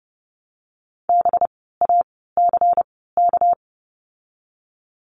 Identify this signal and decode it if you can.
Morse code, heard as a keyed tone, about 32 wpm, spaced as for 17 wpm, 700 Hz. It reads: BACK
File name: course / 30:10wpm / 230 Lesson 37f - Letter x - Words 30wpm